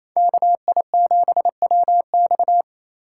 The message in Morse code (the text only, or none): KI7WX